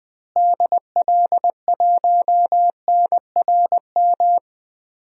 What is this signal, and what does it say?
Morse code, heard as a keyed tone, about 20 wpm, 700 Hz. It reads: DL1NRM
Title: DXCC-100 Call Signs - 6 Characters - Repeated 2x - 20wpm